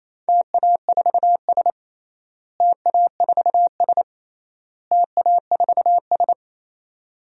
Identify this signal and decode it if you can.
Morse code, heard as a keyed tone, about 28 wpm, 700 Hz. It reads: TA4S TA4S TA4S
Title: DXCC-100 Call Signs - 4 Characters - Repeated 3x - 28wpm